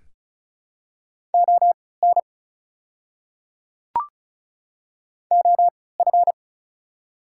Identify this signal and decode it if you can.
Morse code, heard as a keyed tone, about 36 wpm, spaced as for 20 wpm, 700 Hz. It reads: ON OF
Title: 3x Character Spacing - Top 200 Words - 35wpm